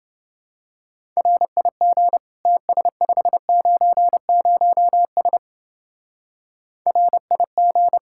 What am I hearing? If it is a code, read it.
Morse code, heard as a keyed tone, about 30 wpm, 700 Hz. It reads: RIG TS590S RIG